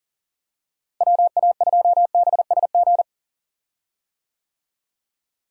Morse code, heard as a keyed tone, about 40 wpm, 700 Hz. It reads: WA2BSG
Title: USA Call Signs - 6 Characters - Repeated 2x - 40wpm